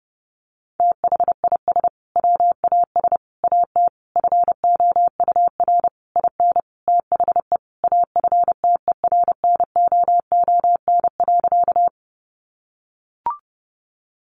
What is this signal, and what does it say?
Morse code, heard as a keyed tone, about 30 wpm, 700 Hz. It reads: THIS WAS AT FOUR IN THE AFTERNOON.
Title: Sentences from Top 1500 Words 30wpm